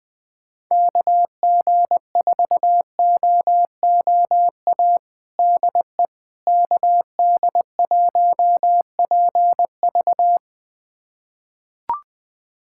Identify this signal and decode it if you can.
Morse code, heard as a keyed tone, about 21 wpm, 700 Hz. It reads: KG4OOA DE KD1PV
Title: USA Callsign DE Callsign 20wpm